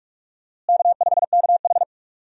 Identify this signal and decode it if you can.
Morse code, heard as a keyed tone, about 42 wpm, 700 Hz. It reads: K5KH
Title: USA Call Signs - 4 Characters - Repeated 2x - 45wpm